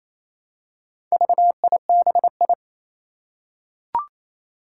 Morse code, heard as a keyed tone, about 27 wpm, 700 Hz. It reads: VIBI
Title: Four Challenging Letter-Number Combinations 28wpm